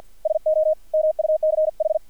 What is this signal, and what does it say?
Morse code, heard as a keyed tone, about 50 wpm, 620 Hz. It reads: SO MUCH
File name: archived / CRSnet